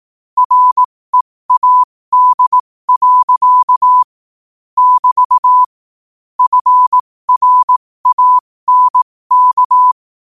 Morse code, heard as a keyed tone, about 18 wpm, spaced as for 15 wpm, 1000 Hz. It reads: READ. <BT> FRANK
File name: qotdfiles / 15wpm